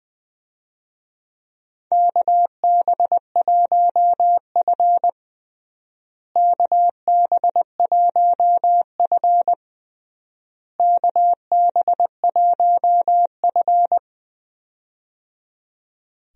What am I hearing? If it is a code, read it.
Morse code, heard as a keyed tone, about 20 wpm, 700 Hz. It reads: KB1F KB1F KB1F
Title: USA Call Signs - 4 Characters - Repeated 3x - 20wpm